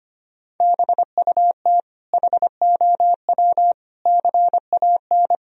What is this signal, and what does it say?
Morse code, heard as a keyed tone, about 25 wpm, 700 Hz. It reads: BUT HOW CAN
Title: Sentences from Top 1000 Words - Review Part 2 - 25wpm